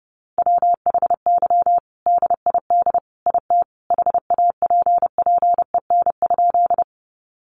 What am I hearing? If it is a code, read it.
Morse code, heard as a keyed tone, about 30 wpm, 700 Hz. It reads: WHY DID IT HAPPEN?
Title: Sentences from Top 400 Words 30wpm